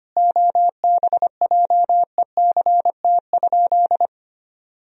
Morse code, heard as a keyed tone, about 25 wpm, 700 Hz. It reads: OBJECT?
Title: Sentences from Top 1000 Words - Review Part 3 - 25wpm